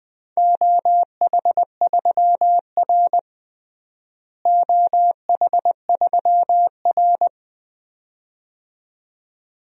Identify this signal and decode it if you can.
Morse code, heard as a keyed tone, about 20 wpm, 700 Hz. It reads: OH3R OH3R
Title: DXCC-100 Call Signs - 4 Characters - Repeated 2x - 20wpm